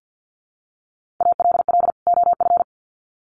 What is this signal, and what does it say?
Morse code, heard as a keyed tone, about 50 wpm, 700 Hz. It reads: ALL OF
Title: Sets of 2 Words 50wpm